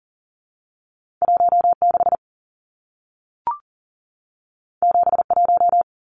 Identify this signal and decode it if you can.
Morse code, heard as a keyed tone, about 40 wpm, 700 Hz. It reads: 16 71
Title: Numbers - 2-Digit 40wpm